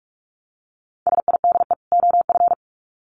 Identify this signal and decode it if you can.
Morse code, heard as a keyed tone, about 44 wpm, 700 Hz. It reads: SIDE OF
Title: Sets of 2 Words 45wpm